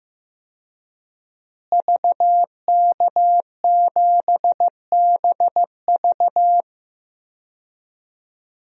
Morse code, heard as a keyed tone, about 15 wpm, 700 Hz. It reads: VK7BV
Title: DXCC-100 Call Signs - 5 Characters - Repeated 2x - 15wpm